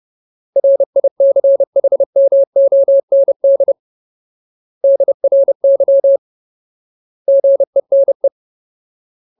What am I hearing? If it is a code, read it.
Morse code, heard as a keyed tone, about 30 wpm, 550 Hz. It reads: RICHMOND DRY GENE
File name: Ultimate Word List / 30WPM Ultimate